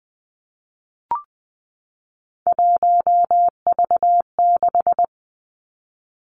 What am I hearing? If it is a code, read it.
Morse code, heard as a keyed tone, about 20 wpm, 700 Hz. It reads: 1V6
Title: Three Challenging Letter-Number Combinations 20wpm